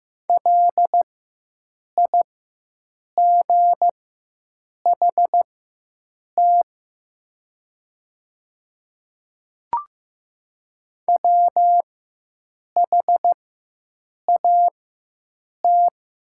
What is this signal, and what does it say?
Morse code, heard as a keyed tone, about 15 wpm, spaced as for 7 wpm, 700 Hz. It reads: LIGHT WHAT
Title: 4x Character Spacing - Top 300 Words - 15wpm